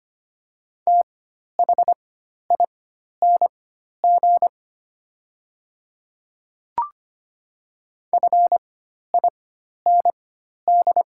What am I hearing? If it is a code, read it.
Morse code, heard as a keyed tone, about 25 wpm, spaced as for 12 wpm, 700 Hz. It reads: THING FIND
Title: 4x Character Spacing - Top 100 Words - 25wpm